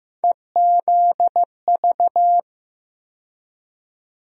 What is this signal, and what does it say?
Morse code, heard as a keyed tone, about 15 wpm, 700 Hz. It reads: EZV
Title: USA Call Sign Postfixes - 3 Letters 15wpm